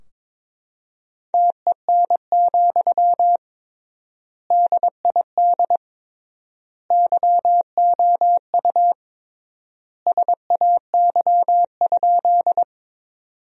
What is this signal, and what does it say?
Morse code, heard as a keyed tone, about 22 wpm, 700 Hz. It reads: TEN, DID YOU SAY?
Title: 3x Word-Spacing - Sentences from Top 400 Words - 22wpm